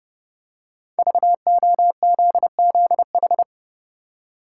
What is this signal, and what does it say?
Morse code, heard as a keyed tone, about 30 wpm, 700 Hz. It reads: VOZZH
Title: Code Groups - 5 Letter:Numbers - 30wpm